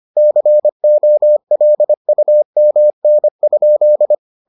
Morse code, heard as a keyed tone, about 25 wpm, 600 Hz. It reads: COLUMN?